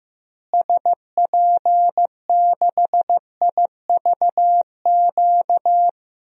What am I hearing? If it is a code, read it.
Morse code, heard as a keyed tone, about 15 wpm, 700 Hz. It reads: SP6IVQ
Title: DXCC-100 Call Signs - 6 Characters - Repeated 2x - 15wpm